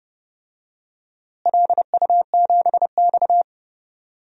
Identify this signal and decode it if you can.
Morse code, heard as a keyed tone, about 32 wpm, 700 Hz. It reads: LU7X
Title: DXCC-100 Call Signs - 4 Characters - Repeated 3x - 30wpm